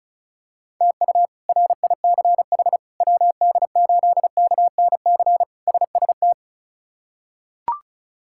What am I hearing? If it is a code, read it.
Morse code, heard as a keyed tone, about 36 wpm, 700 Hz. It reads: TU RICH WD8KNC SST
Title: K1USN SST QSO Practice - 35wpm